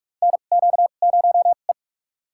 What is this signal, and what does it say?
Morse code, heard as a keyed tone, about 46 wpm, spaced as for 33 wpm, 700 Hz. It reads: NQ0E
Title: USA Call Signs - 4 Characters - 2x Character Spacing - 45wpm